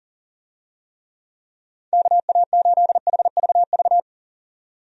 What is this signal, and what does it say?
Morse code, heard as a keyed tone, about 39 wpm, 700 Hz. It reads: KA8HVV
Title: USA Call Signs - 6 Characters - 40wpm